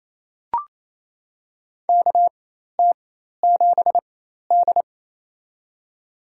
Morse code, heard as a keyed tone, about 28 wpm, spaced as for 13 wpm, 700 Hz. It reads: KT7D
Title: USA Call Signs - 4 Characters - 4x Character Spacing - 28wpm